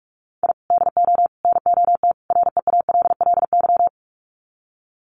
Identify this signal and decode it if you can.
Morse code, heard as a keyed tone, about 44 wpm, 700 Hz. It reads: I DO NOT REALLY
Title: Sets of 4 Words 45wpm